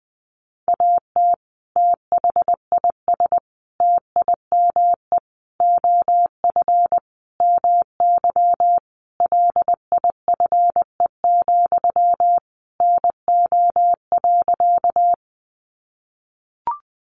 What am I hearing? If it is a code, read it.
Morse code, heard as a keyed tone, about 20 wpm, 700 Hz. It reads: AT THIS TIME OF MY LIFE, NO.
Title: Sentences from Top 300 Words 20wpm